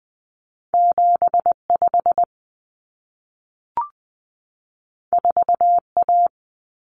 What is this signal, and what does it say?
Morse code, heard as a keyed tone, about 20 wpm, 700 Hz. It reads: 75 4A
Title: Common Cut Numbers - 2-Digit 20wpm